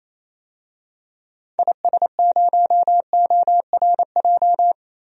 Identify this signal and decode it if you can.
Morse code, heard as a keyed tone, about 28 wpm, 700 Hz. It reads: IS0ORJ
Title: DXCC-100 Call Signs - 6 Characters - Repeated 2x - 28wpm